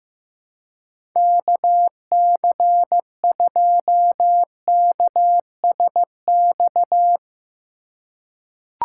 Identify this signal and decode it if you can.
Morse code, heard as a keyed tone, about 15 wpm, 700 Hz. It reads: KC2KSX
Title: USA Call Signs - 6 Characters - 2x Character Spacing - 15wpm